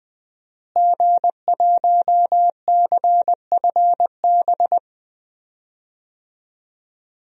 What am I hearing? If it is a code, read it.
Morse code, heard as a keyed tone, about 20 wpm, 700 Hz. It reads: G1CFB
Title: DXCC-100 Call Signs - 5 Characters - Repeated 2x - 20wpm